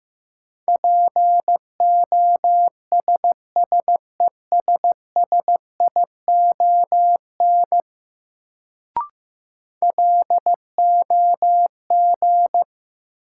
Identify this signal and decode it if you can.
Morse code, heard as a keyed tone, about 16 wpm, 700 Hz. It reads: POSSESSION LOG
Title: Sentences from Top 1400 Words - New Word Drill - 15wpm